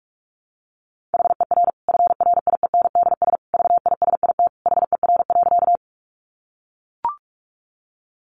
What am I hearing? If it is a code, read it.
Morse code, heard as a keyed tone, about 44 wpm, 700 Hz. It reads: HER FRIENDS VISIT HER.
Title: Easy Phrases 45wpm